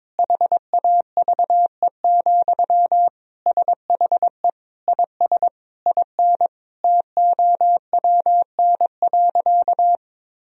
Morse code, heard as a keyed tone, about 22 wpm, 700 Hz. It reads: HAVE, SHE IS IN TOWN.
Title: Sentences from Top 1000 Words - Review Part 4 - 22wpm